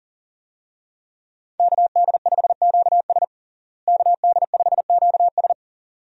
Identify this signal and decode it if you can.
Morse code, heard as a keyed tone, about 40 wpm, 700 Hz. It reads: KD5QS KD5QS
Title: USA Call Signs - 5 Characters - Repeated 2x - 40wpm